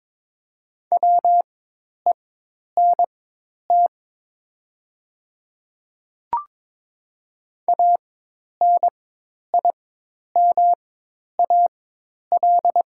Morse code, heard as a keyed tone, about 22 wpm, spaced as for 10 wpm, 700 Hz. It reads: WENT ANIMAL
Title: 4x Character Spacing - Top 300 Words - 22wpm